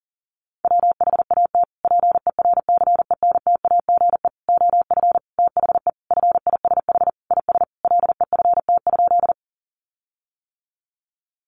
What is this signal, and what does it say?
Morse code, heard as a keyed tone, about 37 wpm, 700 Hz. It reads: WHAT PERCENTAGE OF THE FISH IS LEFT?
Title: Sentences from Top 1300 Words 40wpm